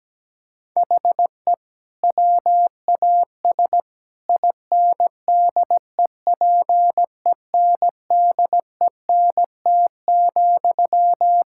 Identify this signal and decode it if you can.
Morse code, heard as a keyed tone, about 17 wpm, 700 Hz. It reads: HE WAS INDEPENDENT,
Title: Sentences from Top 1000 Words - Review Part 6 - 17wpm